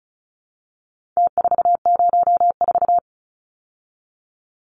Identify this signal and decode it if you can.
Morse code, heard as a keyed tone, about 34 wpm, 700 Hz. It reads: T404
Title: Numbers - Contest Style - 4-Digit 35wpm